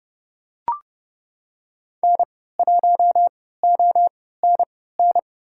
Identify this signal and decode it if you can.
Morse code, heard as a keyed tone, about 32 wpm, spaced as for 17 wpm, 700 Hz. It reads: N1ONN
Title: USA Call Signs - 5 Characters - 3x Character Spacing - 30wpm